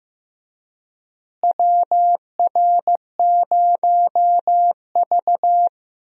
Morse code, heard as a keyed tone, about 15 wpm, 700 Hz. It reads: WR0V
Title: USA Call Signs - 4 Characters - Repeated 2x - 15wpm